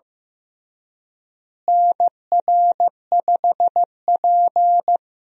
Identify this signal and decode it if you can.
Morse code, heard as a keyed tone, about 15 wpm, 700 Hz. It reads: NR5P
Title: USA Call Signs - 4 Characters - Repeated 3x - 15wpm